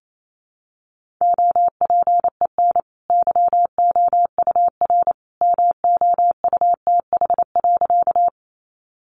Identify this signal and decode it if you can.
Morse code, heard as a keyed tone, about 28 wpm, 700 Hz. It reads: OPEN YOUR MOUTH.
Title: Sentences from Top 1500 Words 28wpm